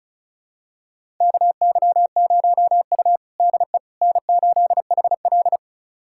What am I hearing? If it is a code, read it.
Morse code, heard as a keyed tone, about 36 wpm, 700 Hz. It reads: KY0U DE N8HL
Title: USA Callsign DE Callsign - 4 Characters - 35wpm